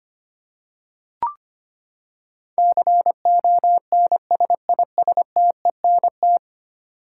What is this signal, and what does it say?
Morse code, heard as a keyed tone, about 25 wpm, 700 Hz. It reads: CONSISTENT
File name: Ten Letter Words - 25wpm